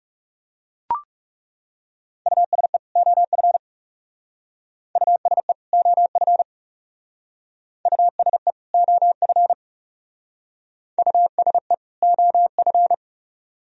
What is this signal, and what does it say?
Morse code, heard as a keyed tone, about 36 wpm, 700 Hz. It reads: USE OF USE OF USE OF USE OF
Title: Speed-Racing - Sets of 2 Words 30wpm